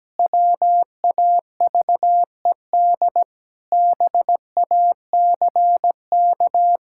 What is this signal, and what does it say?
Morse code, heard as a keyed tone, about 17 wpm, 700 Hz. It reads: WAVED BACK
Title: Pangram Challenge v2 - 17wpm